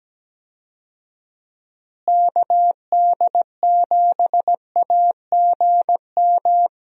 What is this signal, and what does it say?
Morse code, heard as a keyed tone, about 17 wpm, 700 Hz. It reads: KD7AGM
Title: USA Call Signs - 6 Characters - Repeated 2x - 17wpm